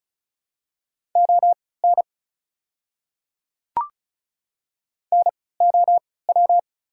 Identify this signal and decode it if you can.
Morse code, heard as a keyed tone, about 34 wpm, spaced as for 20 wpm, 700 Hz. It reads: ON NOW
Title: 3x Character Spacing - Top 100 Words - 35wpm